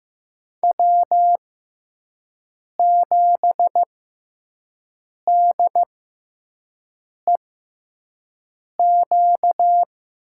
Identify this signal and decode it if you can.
Morse code, heard as a keyed tone, about 15 wpm, spaced as for 5 wpm, 700 Hz. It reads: W7DEQ
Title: USA Call Signs - 5 Characters - 6x Character Spacing - 15wpm